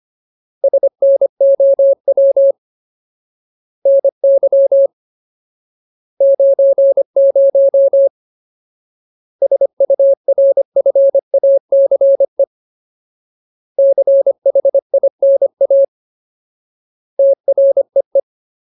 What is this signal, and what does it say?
Morse code, heard as a keyed tone, about 25 wpm, 550 Hz. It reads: SNOW NY 90 SURFACE CHINA TREE
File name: Ultimate Word List / 25WPM Ultimate